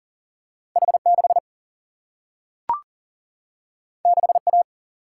Morse code, heard as a keyed tone, about 40 wpm, 700 Hz. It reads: H6 6A
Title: Single-Letter Single-Number 40wpm